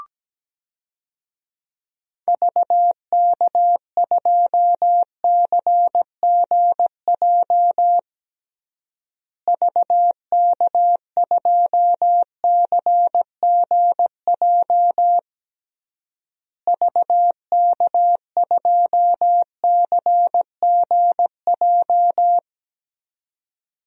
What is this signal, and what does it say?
Morse code, heard as a keyed tone, about 17 wpm, 700 Hz. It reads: VK2CGJ VK2CGJ VK2CGJ
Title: DXCC-100 Call Signs - 6 Characters - Repeated 3x - 17wpm